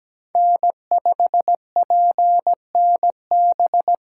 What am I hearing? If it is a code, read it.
Morse code, heard as a keyed tone, about 20 wpm, 700 Hz. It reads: N5PNB
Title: Speed Racing - USA 1x3 Call Signs 17wpm